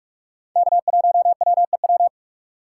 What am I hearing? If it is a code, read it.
Morse code, heard as a keyed tone, about 46 wpm, 700 Hz. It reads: K1WEW
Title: USA Call Signs - 5 Characters - 6x Character Spacing - 45wpm